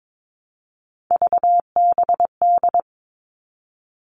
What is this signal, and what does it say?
Morse code, heard as a keyed tone, about 22 wpm, 700 Hz. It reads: VBD
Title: All Cut Numbers - 3-Digit 22wpm